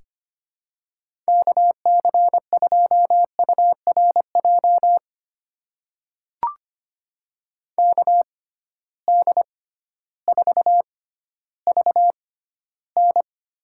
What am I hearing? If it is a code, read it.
Morse code, heard as a keyed tone, about 25 wpm, spaced as for 13 wpm, 700 Hz. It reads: KC2URJ KD4VN
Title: USA Call Signs - 6 Characters - 6x Character Spacing - 25wpm